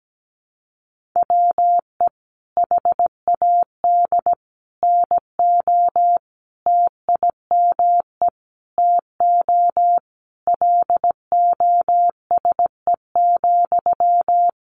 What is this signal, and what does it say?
Morse code, heard as a keyed tone, about 17 wpm, 700 Hz. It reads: WE HAD NO TIME TO LOSE,
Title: Sentences from Top 1100 Words 17wpm